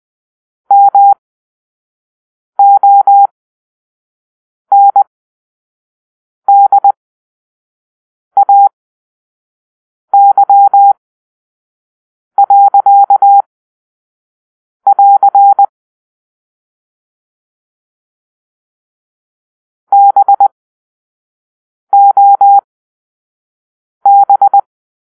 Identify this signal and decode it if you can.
Morse code, heard as a keyed tone, about 20 wpm, spaced as for 5 wpm, 800 Hz. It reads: MONDAY.<AR> BOB